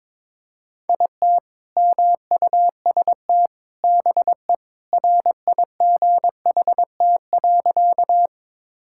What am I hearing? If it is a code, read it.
Morse code, heard as a keyed tone, about 22 wpm, 700 Hz. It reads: IT MUST BE RIGHT.
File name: practice / 22wpm / Sentences with 2-Word Phrases - 22wpm